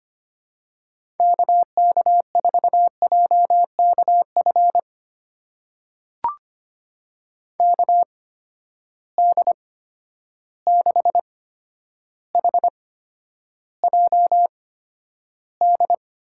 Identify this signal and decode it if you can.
Morse code, heard as a keyed tone, about 25 wpm, spaced as for 11 wpm, 700 Hz. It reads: KK4JKF KD6HJD
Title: USA Call Signs - 6 Characters - 8x Character Spacing - 25wpm